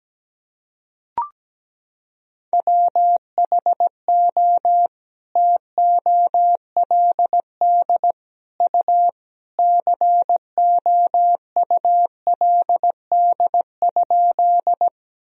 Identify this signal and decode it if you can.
Morse code, heard as a keyed tone, about 17 wpm, 700 Hz. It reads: WHO TOLD U COULD?
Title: Sentences from Top 400 Words and Abbreviations 17wpm